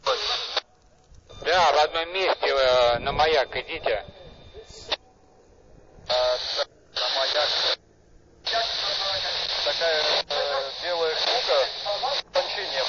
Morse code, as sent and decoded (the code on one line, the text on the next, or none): none